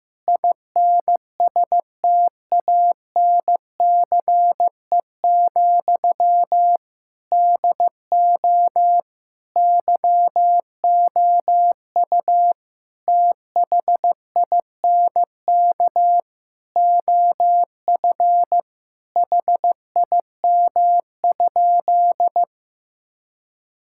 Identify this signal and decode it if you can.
Morse code, heard as a keyed tone, about 15 wpm, 700 Hz. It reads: INSTANCE, DO YOU THINK OF HIM?
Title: Sentences from Top 1000 Words - Review Part 5 - 15wpm